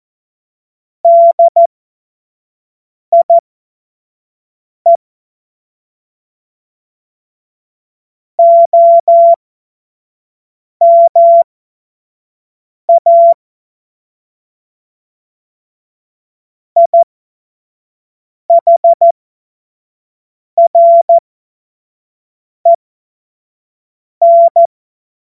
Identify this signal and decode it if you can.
Morse code, heard as a keyed tone, about 14 wpm, spaced as for 5 wpm, 680 Hz. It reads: DIE OMA IHREN
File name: Morsekurs / Witze / Witz 12. Oktober 2021, 25 F70